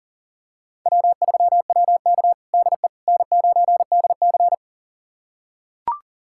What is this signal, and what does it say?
Morse code, heard as a keyed tone, about 40 wpm, 700 Hz. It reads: W3WK DE N9DC
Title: USA Callsign DE Callsign - 4 Characters - 40wpm